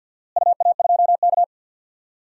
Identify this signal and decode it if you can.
Morse code, heard as a keyed tone, about 45 wpm, 700 Hz. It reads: UA2K
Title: DXCC-100 Call Signs - 4 Characters - Repeated 3x - 50wpm